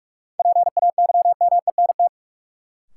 Morse code, heard as a keyed tone, about 46 wpm, 700 Hz. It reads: PAYMENT
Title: Sentences from Top 1200 Words - New Word Drill - 45wpm